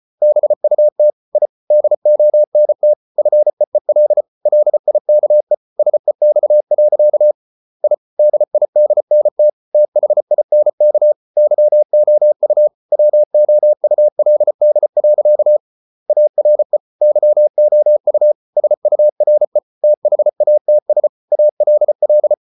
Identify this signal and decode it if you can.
Morse code, heard as a keyed tone, about 34 wpm, 600 Hz. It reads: BUT I DONT FEEL LIKE SEX. I DIDNT THINK YOU WOULD. ARE YOU SURE THATS ALL